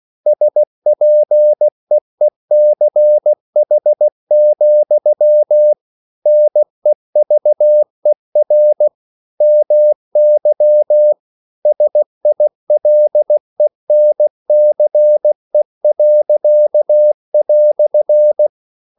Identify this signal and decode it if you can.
Morse code, heard as a keyed tone, about 16 wpm, 600 Hz. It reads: SPEECH, NEVER MY SILENCE."
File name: CW / Practice / EnglishQuotations0216